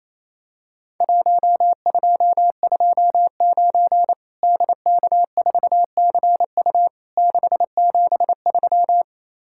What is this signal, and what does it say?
Morse code, heard as a keyed tone, about 28 wpm, 700 Hz. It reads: 1229 DK4CU 673
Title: Worked All Europe DX Contest - QTC Exchange - 28wpm